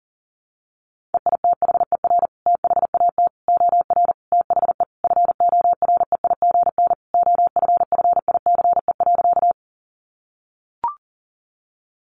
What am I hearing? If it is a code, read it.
Morse code, heard as a keyed tone, about 40 wpm, 700 Hz. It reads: EITHER THAT OR THE FOREIGN OFFICE.
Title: Sentences from Top 1400 Words 40wpm